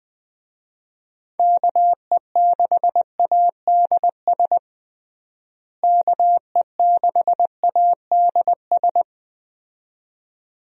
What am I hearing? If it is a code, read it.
Morse code, heard as a keyed tone, about 20 wpm, 700 Hz. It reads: KE6ADS KE6ADS
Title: USA Call Signs - 6 Characters - Repeated 2x - 20wpm